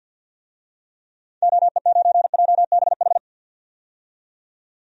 Morse code, heard as a keyed tone, about 52 wpm, 700 Hz. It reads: OE9JBH
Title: DXCC-100 Call Signs - 6 Characters - 50wpm